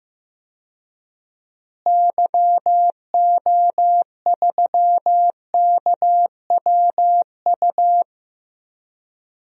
Morse code, heard as a keyed tone, about 15 wpm, 700 Hz. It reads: YO3KWU
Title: DXCC-100 Call Signs - 6 Characters - Repeated 2x - 15wpm